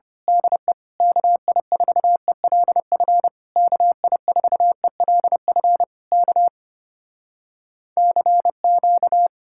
Morse code, heard as a keyed tone, about 25 wpm, 700 Hz. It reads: DE KI4ELF KI4ELF K CQ
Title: Speed-Racing Common QSO Phrases - 25wpm